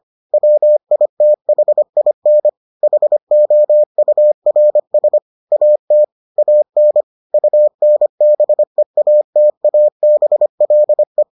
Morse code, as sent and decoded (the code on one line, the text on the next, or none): .-- .. - .... .. -. / .... --- ..- .-. ... / .- - / .- -. / ..- -. -... .- - .- -... .-.. .
WITHIN HOURS AT AN UNBEATABLE